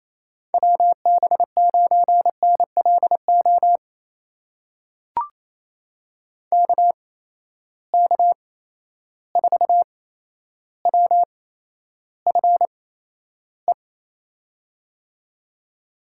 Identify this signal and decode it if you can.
Morse code, heard as a keyed tone, about 28 wpm, spaced as for 12 wpm, 700 Hz. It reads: WB9NLO KK4WFE
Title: USA Call Signs - 6 Characters - 8x Character Spacing - 28wpm